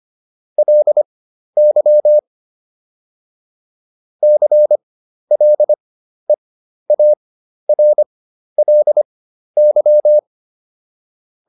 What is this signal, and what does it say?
Morse code, heard as a keyed tone, about 25 wpm, spaced as for 12 wpm, 600 Hz. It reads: LY CLEARLY